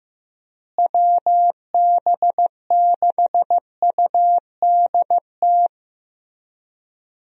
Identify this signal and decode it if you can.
Morse code, heard as a keyed tone, about 15 wpm, 700 Hz. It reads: WB6UDT
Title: USA Call Signs - 6 Characters - Repeated 3x - 15wpm